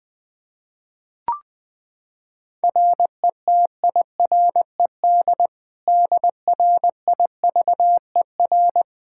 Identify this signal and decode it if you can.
Morse code, heard as a keyed tone, about 20 wpm, 700 Hz. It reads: RETIRED DRIVER
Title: Practice QSO - Exchange 3 - More Info - 20wpm